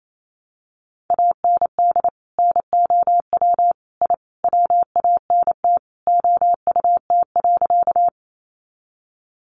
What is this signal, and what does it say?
Morse code, heard as a keyed tone, about 28 wpm, 700 Hz. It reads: AND NOW I WANT OUT.
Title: A Sound Of Thunder 28wpm